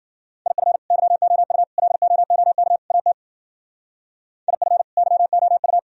Morse code, heard as a keyed tone, about 49 wpm, 700 Hz. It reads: IF YOU LOOK AT IF YOU
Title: Speed-Racing - Sets of 4 Words 40wpm